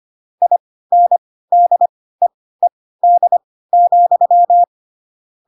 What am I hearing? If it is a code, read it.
Morse code, heard as a keyed tone, about 25 wpm, spaced as for 16 wpm, 700 Hz. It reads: INDEED,
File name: Pinocchio / Pinocchio ch2 10 25 ik1hge